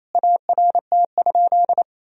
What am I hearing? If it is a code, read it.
Morse code, heard as a keyed tone, about 28 wpm, 700 Hz. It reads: ART?
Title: Sentences from Top 1000 Words - Review Part 3 - 28wpm